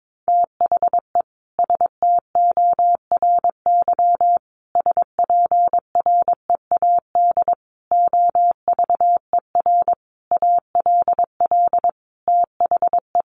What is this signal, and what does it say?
Morse code, heard as a keyed tone, about 22 wpm, 700 Hz. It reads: THE STORY SPREAD OVER ALL THE
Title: Sentences from Top 1200 Words 22wpm